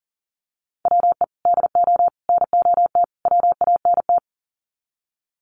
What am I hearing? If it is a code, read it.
Morse code, heard as a keyed tone, about 40 wpm, 700 Hz. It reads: WE DO NOT WANT
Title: Sets of 4 Words 40wpm